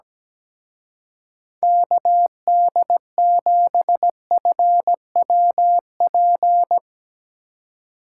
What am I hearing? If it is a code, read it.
Morse code, heard as a keyed tone, about 17 wpm, 700 Hz. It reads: KD7FWP
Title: Speed Racing - USA 2x3 Call Signs 17wpm